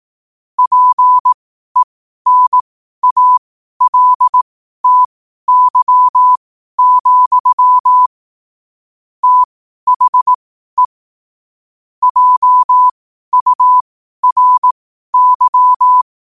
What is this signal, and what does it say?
Morse code, heard as a keyed tone, about 18 wpm, spaced as for 12 wpm, 1000 Hz. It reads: PENALTY, THE JURY
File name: qotdfiles / 12wpm